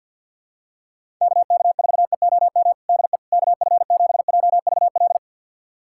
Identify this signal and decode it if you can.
Morse code, heard as a keyed tone, about 52 wpm, 700 Hz. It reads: KK4EOM DE KF7JVL
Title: USA Callsign DE Callsign - 6 Characters - 50wpm